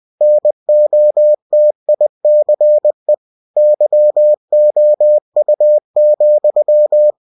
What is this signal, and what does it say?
Morse code, heard as a keyed tone, about 20 wpm, 600 Hz. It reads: NOTICE YOU,
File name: cw-books / winnie-the-pooh / winnie-20wpm-page-0005